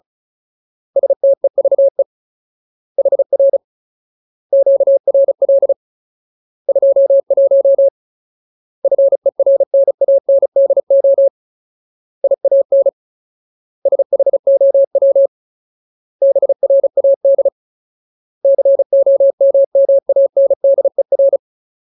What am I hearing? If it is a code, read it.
Morse code, heard as a keyed tone, about 34 wpm, 550 Hz. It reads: STEVE HR QRL 21 FERNANDO IAN SHOW BRAD COMMANDER
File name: Ultimate Word List / 35WPM Ultimate